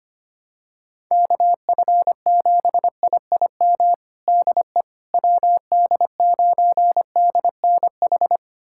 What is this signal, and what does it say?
Morse code, heard as a keyed tone, about 25 wpm, 700 Hz. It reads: KF7IIM DE WD9DNH
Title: USA Callsign DE Callsign - 6 Characters - 25wpm